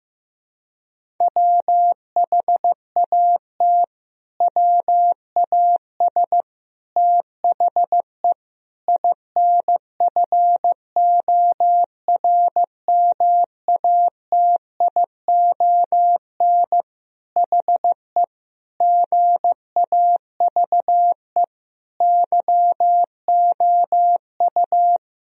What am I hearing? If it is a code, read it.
Morse code, heard as a keyed tone, about 15 wpm, 700 Hz. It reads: WHAT WAS THE INFORMATION HE GAVE YOU
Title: Sentences from Top 1000 Words - Review Part 3 - 15wpm